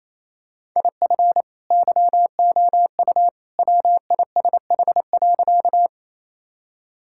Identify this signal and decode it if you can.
Morse code, heard as a keyed tone, about 27 wpm, 700 Hz. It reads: IF YOU WISH.
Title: Sentences from Top 1600 Words 28wpm